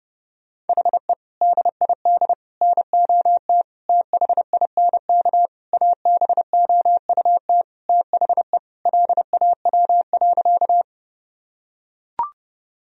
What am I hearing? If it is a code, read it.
Morse code, heard as a keyed tone, about 30 wpm, 700 Hz. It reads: HE DID NOT THINK ABOUT THE LAW.
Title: Sentences from Top 1000 Words - Review Part 3 - 30wpm